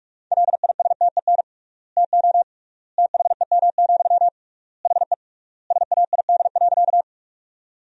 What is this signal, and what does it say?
Morse code, heard as a keyed tone, about 46 wpm, 700 Hz. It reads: LISTEN TO THEM, HE SAID.
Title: 3x Word-Spacing - Sentences from Top 400 Words - 45wpm